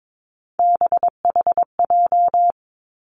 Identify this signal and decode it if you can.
Morse code, heard as a keyed tone, about 22 wpm, 700 Hz. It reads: BHJ
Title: Three Challenging Letter-Number Combinations 22wpm